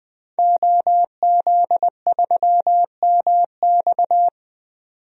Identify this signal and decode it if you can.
Morse code, heard as a keyed tone, about 20 wpm, 700 Hz. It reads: OZ3MX
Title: DXCC-100 Call Signs - 5 Characters - 20wpm